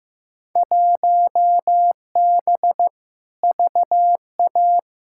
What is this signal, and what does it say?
Morse code, heard as a keyed tone, about 15 wpm, 700 Hz. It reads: 1B VA
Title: Field Day Exchanges - 15wpm